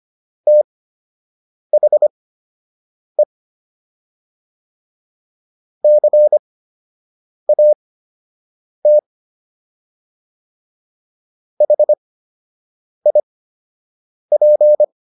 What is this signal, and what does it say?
Morse code, heard as a keyed tone, about 25 wpm, spaced as for 7 wpm, 600 Hz. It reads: THE CAT HIP